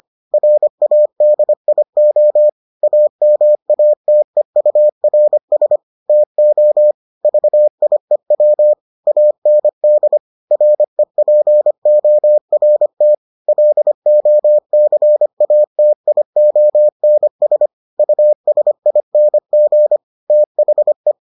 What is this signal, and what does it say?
Morse code, heard as a keyed tone, about 25 wpm, 600 Hz. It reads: RADIO AMATEURS TO VIEW AND REPORT LOCATIONS USING THE